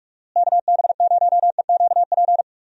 Morse code, heard as a keyed tone, about 41 wpm, 700 Hz. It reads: KB0EQP
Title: USA Call Signs - 6 Characters - Repeated 2x - 45wpm